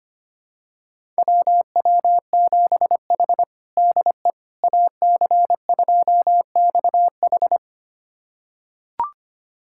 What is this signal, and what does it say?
Morse code, heard as a keyed tone, about 25 wpm, 700 Hz. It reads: WW7H DE AC2XH
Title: USA Callsign DE Callsign 25wpm